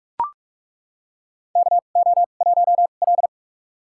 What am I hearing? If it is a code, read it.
Morse code, heard as a keyed tone, about 42 wpm, spaced as for 31 wpm, 700 Hz. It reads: KO1L
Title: USA Call Signs - 4 Characters - 2x Character Spacing - 45wpm